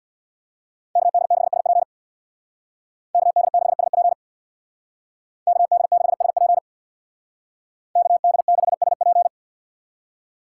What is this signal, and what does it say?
Morse code, heard as a keyed tone, about 63 wpm, 700 Hz. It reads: KD6SP KD6SP KD6SP KD6SP